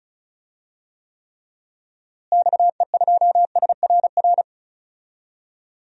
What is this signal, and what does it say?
Morse code, heard as a keyed tone, about 36 wpm, 700 Hz. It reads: XE2SRR
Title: DXCC-100 Call Signs - 6 Characters - Repeated 3x - 35wpm